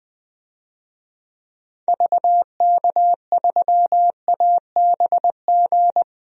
Morse code, heard as a keyed tone, about 20 wpm, 700 Hz. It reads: VK3ABG
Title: DXCC-100 Call Signs - 6 Characters - 20wpm